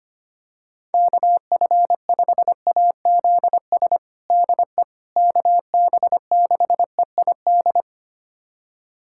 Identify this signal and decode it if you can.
Morse code, heard as a keyed tone, about 25 wpm, 700 Hz. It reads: KF5AZS DE KB6EID
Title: USA Callsign DE Callsign - 6 Characters - 25wpm